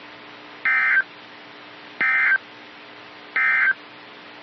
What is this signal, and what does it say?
Emergency Alert System (SAME) tones, heard as harsh buzzing bursts. It is heard from 0.5 to 3.8 s.
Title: NOAA Weather Radio - KEC55